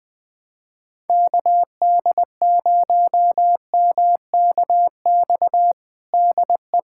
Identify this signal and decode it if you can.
Morse code, heard as a keyed tone, about 20 wpm, 700 Hz. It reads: KD0MKX DE
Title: USA Callsign DE Callsign - 6 Characters - 20wpm